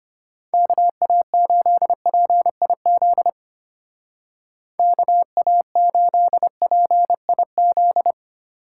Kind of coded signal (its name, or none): Morse code